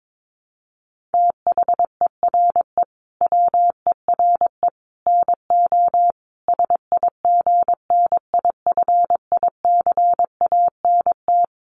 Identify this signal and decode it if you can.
Morse code, heard as a keyed tone, about 22 wpm, 700 Hz. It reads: THERE WERE NO SIGNIFICANT
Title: Sets of 5 Words Encore2 22wpm